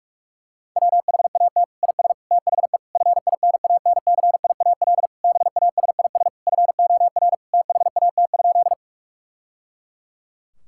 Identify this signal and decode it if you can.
Morse code, heard as a keyed tone, about 41 wpm, 700 Hz. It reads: WHAT IS THE FINANCIAL BASIS FOR THAT?
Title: Sentences from Top 1000 Words - Review Part 5 - 45wpm